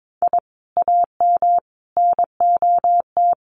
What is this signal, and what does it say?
Morse code, heard as a keyed tone, about 22 wpm, 700 Hz. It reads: I AM NOT
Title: Sets of 5 Words Encore 22wpm